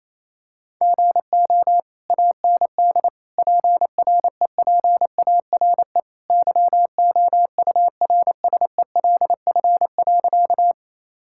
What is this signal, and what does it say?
Morse code, heard as a keyed tone, about 27 wpm, 700 Hz. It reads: GO AND PREPARE YOURSELF.
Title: Sentences from Top 1000 Words - Review Part 6 - 28wpm